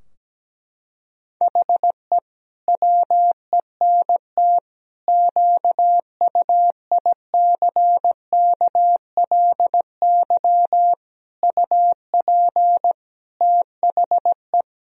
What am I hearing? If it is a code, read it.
Morse code, heard as a keyed tone, about 17 wpm, 700 Hz. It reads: HE WENT QUICKLY UP THE
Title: Sentences from Top 1000 Words - Review Part 2 - 17wpm